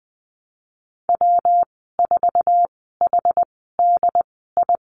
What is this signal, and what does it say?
Morse code, heard as a keyed tone, about 20 wpm, spaced as for 15 wpm, 700 Hz. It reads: W4HDI
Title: 122 Lesson 20g - Number 3 - Call Signs 20wpm